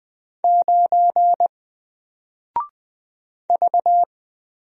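Morse code, heard as a keyed tone, about 20 wpm, 700 Hz. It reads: 9 V
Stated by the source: Single letter and number Morse drill at 20 wpm, from N4LCD.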